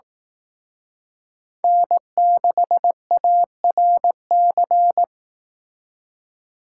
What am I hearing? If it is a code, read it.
Morse code, heard as a keyed tone, about 18 wpm, 700 Hz. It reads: N6ARC
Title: Speed Racing - USA 1x3 Call Signs 15wpm